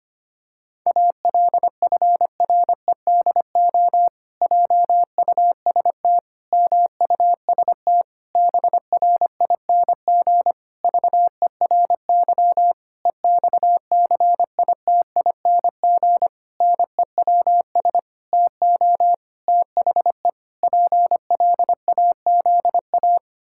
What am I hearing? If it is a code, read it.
Morse code, heard as a keyed tone, about 25 wpm, 700 Hz. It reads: ALFREDO JUST MUST BRING VERY EXCITING NEWS TO THE PLAZA
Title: Pangram Challenge v2 - 25wpm